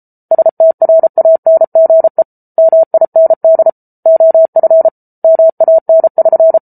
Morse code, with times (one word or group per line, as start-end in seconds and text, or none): 0.3-2.2 STRANGE
2.6-3.7 MIND
4.1-4.9 OF
5.2-6.6 MAN<SN>